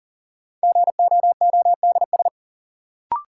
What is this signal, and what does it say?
Morse code, heard as a keyed tone, about 40 wpm, 700 Hz. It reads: GOODS
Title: Sentences from Top 1000 Words - Review Part 6 - New Word Drill - 40wpm